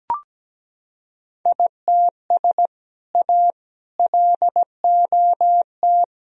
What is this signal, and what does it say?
Morse code, heard as a keyed tone, about 17 wpm, 700 Hz. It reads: ITS A LOT
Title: Sentences from Top 1000 Words - Review Part 3 - 17wpm